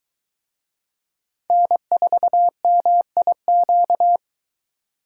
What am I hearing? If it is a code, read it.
Morse code, heard as a keyed tone, about 22 wpm, 700 Hz. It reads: N4MIQ